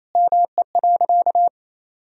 Morse code, heard as a keyed tone, about 28 wpm, 700 Hz. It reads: ME.
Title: Sentences from Top 400 Words and Abbreviations 28wpm